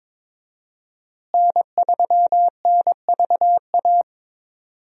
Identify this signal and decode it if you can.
Morse code, heard as a keyed tone, about 22 wpm, 700 Hz. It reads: N3NVA
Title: USA Call Signs - 5 Characters - Repeated 3x - 22wpm